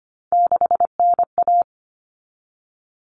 Morse code, heard as a keyed tone, about 25 wpm, 700 Hz. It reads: 6NA
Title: Common Cut Numbers - 3-Digit 25wpm